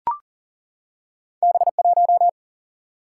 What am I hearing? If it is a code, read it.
Morse code, heard as a keyed tone, about 41 wpm, 700 Hz. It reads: B1
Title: Single-Letter Single-Number 40wpm